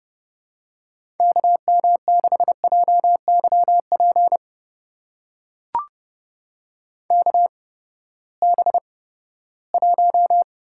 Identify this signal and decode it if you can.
Morse code, heard as a keyed tone, about 32 wpm, spaced as for 17 wpm, 700 Hz. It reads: KM6JYP KB1
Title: USA Call Signs - 6 Characters - 8x Character Spacing - 30wpm